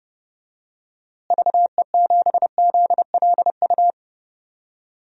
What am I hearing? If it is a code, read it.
Morse code, heard as a keyed tone, about 30 wpm, 700 Hz. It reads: VE7ZLU
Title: DXCC-100 Call Signs - 6 Characters - Repeated 3x - 30wpm